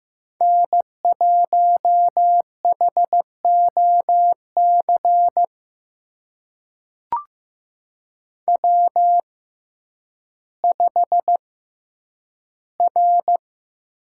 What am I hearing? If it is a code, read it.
Morse code, heard as a keyed tone, about 15 wpm, spaced as for 9 wpm, 700 Hz. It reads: N1HOC W5R
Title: USA Call Signs - 5 Characters - 6x Character Spacing - 15wpm